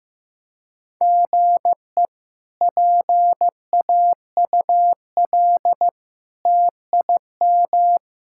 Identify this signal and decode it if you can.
Morse code, heard as a keyed tone, about 15 wpm, 700 Hz. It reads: GE PAUL TIM
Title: K1USN SST QSO Practice - 15wpm